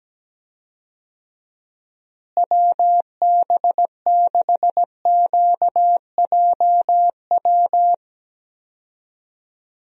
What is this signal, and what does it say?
Morse code, heard as a keyed tone, about 17 wpm, 700 Hz. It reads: WB6QJW
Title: USA Call Signs - 6 Characters - Repeated 3x - 17wpm